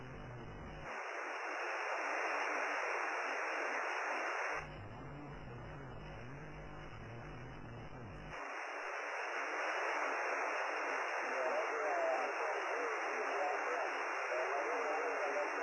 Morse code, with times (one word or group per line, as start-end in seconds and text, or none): none